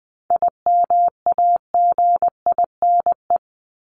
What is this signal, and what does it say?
Morse code, heard as a keyed tone, about 20 wpm, 700 Hz. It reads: IMAGINE
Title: Top 1251-1500 Words 20wpm